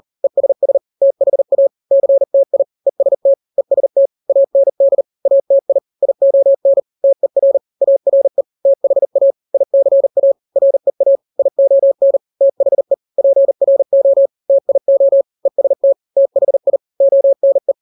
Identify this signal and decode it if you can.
Morse code, heard as a keyed tone, about 40 wpm, 550 Hz. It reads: ESS THA CTI EST EST AND ATI ION TER ARE THA ICA REA ION THE PRO TIO EST THI ONE
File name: n-grams / 3-grams 40 35 WPM